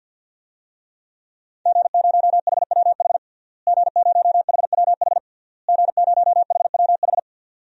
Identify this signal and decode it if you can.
Morse code, heard as a keyed tone, about 52 wpm, 700 Hz. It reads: G0HWH G0HWH G0HWH
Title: DXCC-100 Call Signs - 5 Characters - Repeated 3x - 50wpm